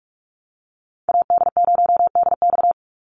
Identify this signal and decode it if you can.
Morse code, heard as a keyed tone, about 44 wpm, 700 Hz. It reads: AD0DX
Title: Call Signs - 45wpm